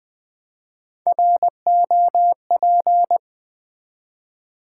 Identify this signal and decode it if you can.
Morse code, heard as a keyed tone, about 20 wpm, 700 Hz. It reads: ROP
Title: Build A Word - 1 Letter At A Time - 4-Letter Words - 20wpm